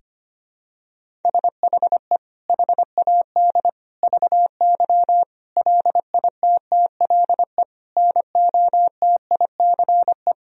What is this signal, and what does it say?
Morse code, heard as a keyed tone, about 25 wpm, 700 Hz. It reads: SHE HAD VY LITTLE NOTICE
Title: Sentences from Top 400 Words and Abbreviations 25wpm